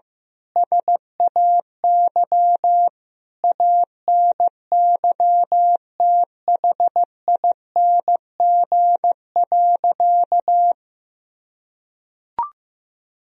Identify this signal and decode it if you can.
Morse code, heard as a keyed tone, about 15 wpm, 700 Hz. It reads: SAY ANYTHING.
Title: Sentences from Top 1000 Words - Review Part 2 - 15wpm